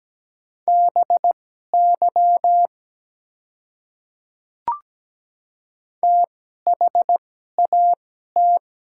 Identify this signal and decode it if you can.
Morse code, heard as a keyed tone, about 17 wpm, spaced as for 12 wpm, 700 Hz. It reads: BY THAT
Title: 2x Character Spacing - Top 100 Words - 17wpm